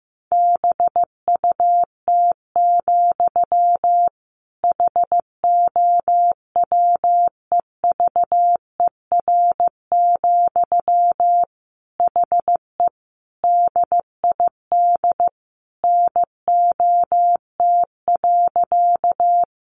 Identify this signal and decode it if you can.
Morse code, heard as a keyed tone, about 15 wpm, 700 Hz. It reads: BUT, HOWEVER, HE DID NOT.
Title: Sentences from Top 600 Words 15wpm